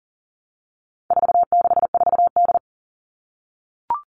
Morse code, heard as a keyed tone, about 40 wpm, 700 Hz. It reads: 464D
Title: All Cut Numbers - 4-Digit 40wpm